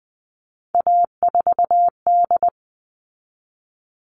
Morse code, heard as a keyed tone, about 20 wpm, 700 Hz. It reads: A4D
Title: All Cut Numbers - 3-Digit 20wpm